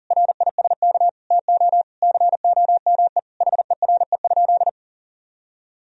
Morse code, heard as a keyed tone, about 37 wpm, 700 Hz. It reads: RISK TO COME HERE?
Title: Sentences from Top 1000 Words - Review Part 4 - 40wpm